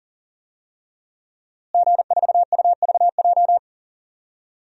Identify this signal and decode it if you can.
Morse code, heard as a keyed tone, about 40 wpm, 700 Hz. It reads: G4UVJ
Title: DXCC-100 Call Signs - 5 Characters - Repeated 3x - 40wpm